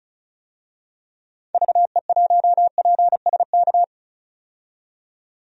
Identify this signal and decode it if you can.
Morse code, heard as a keyed tone, about 36 wpm, 700 Hz. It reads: VE1PSK
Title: DXCC-100 Call Signs - 6 Characters - 35wpm